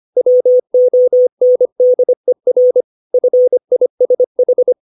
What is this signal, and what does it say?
Morse code, heard as a keyed tone, about 25 wpm, 500 Hz. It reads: WONDER FISH
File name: 632 words / 632 Words 25WPM